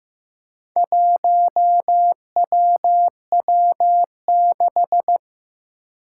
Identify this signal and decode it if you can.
Morse code, heard as a keyed tone, about 15 wpm, 700 Hz. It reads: 1WW6
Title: Four Challenging Letter-Number Combinations 15wpm